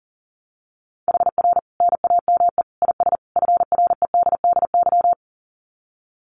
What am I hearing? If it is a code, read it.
Morse code, heard as a keyed tone, about 40 wpm, 700 Hz. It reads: HR NAME IS FREDDY
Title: Common QSO Phrases 40wpm